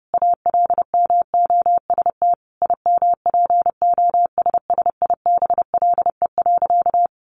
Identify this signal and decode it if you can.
Morse code, heard as a keyed tone, about 30 wpm, 700 Hz. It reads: ALMOST IMPOSSIBLE.
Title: Sentences from Top 1100 Words 30wpm